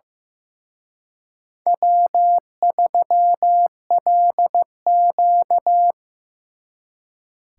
Morse code, heard as a keyed tone, about 17 wpm, 700 Hz. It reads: W3LQ
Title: Speed Racing - USA 1x2 Call Signs 15wpm